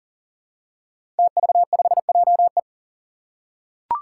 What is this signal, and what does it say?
Morse code, heard as a keyed tone, about 40 wpm, 700 Hz. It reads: TV5JE
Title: Five Challenging Letter-Number Combinations 40wpm